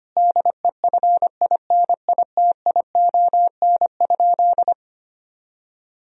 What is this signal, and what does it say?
Morse code, heard as a keyed tone, about 25 wpm, 700 Hz. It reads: DEFINITION?
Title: Sentences from Top 1000 Words - Review Part 4 - 25wpm